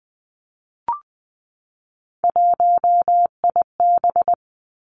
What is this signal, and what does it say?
Morse code, heard as a keyed tone, about 20 wpm, 700 Hz. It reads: 1IB
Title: Three Challenging Letter-Number Combinations 20wpm